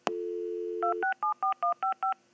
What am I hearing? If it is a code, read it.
DTMF, heard as telephone touch tones, tones of about 120 ms. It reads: 2674155
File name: machine sound effects / dial-up-modem-2